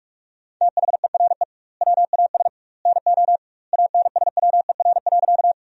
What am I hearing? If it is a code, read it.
Morse code, heard as a keyed tone, about 46 wpm, 700 Hz. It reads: THERE WAS NO ANSWER.
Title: 2x Word-Spacing - Sentences from Top 300 Words - 45wpm